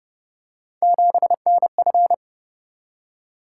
Morse code, heard as a keyed tone, about 29 wpm, 700 Hz. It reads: 7NF